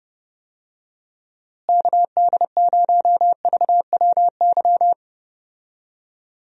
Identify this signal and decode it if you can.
Morse code, heard as a keyed tone, about 31 wpm, 700 Hz. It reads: KD0VWY
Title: USA Call Signs - 6 Characters - Repeated 2x - 30wpm